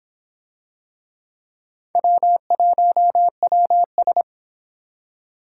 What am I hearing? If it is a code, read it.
Morse code, heard as a keyed tone, about 23 wpm, 700 Hz. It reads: W1WS